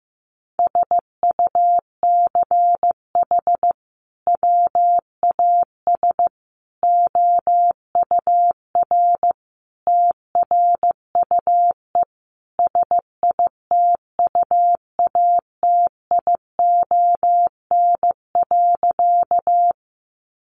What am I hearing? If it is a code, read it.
Morse code, heard as a keyed tone, about 15 wpm, 700 Hz. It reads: SUCH WAS OUR TRUE SITUATION.
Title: Sentences from Top 1200 Words 15wpm